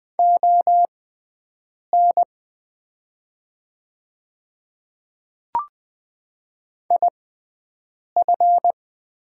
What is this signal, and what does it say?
Morse code, heard as a keyed tone, about 20 wpm, spaced as for 7 wpm, 700 Hz. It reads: ON IF
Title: 6x Character Spacing - Top 100 Words - 20wpm